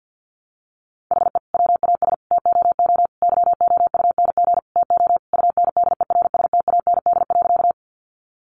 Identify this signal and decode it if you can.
Morse code, heard as a keyed tone, about 50 wpm, 700 Hz. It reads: HE WAS TOO YOUNG TO UNDERSTAND.
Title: Sentences from Top 600 Words 50wpm